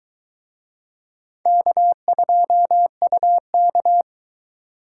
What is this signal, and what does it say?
Morse code, heard as a keyed tone, about 23 wpm, 700 Hz. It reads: K2UK